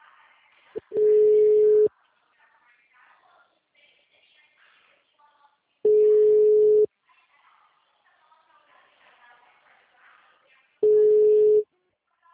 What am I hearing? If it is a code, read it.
Ringback tone (Europe, 425 Hz), with speech.